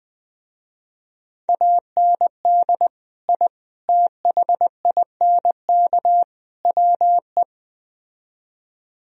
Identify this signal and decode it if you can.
Morse code, heard as a keyed tone, about 20 wpm, 700 Hz. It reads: AND I THINK WE